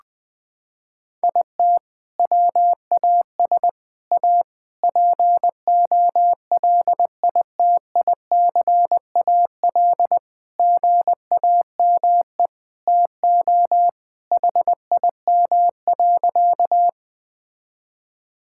Morse code, heard as a keyed tone, about 20 wpm, 700 Hz. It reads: IT WAS A POLITICAL GAME TO HIM.
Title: Sentences from Top 1000 Words - Review Part 5 - 20wpm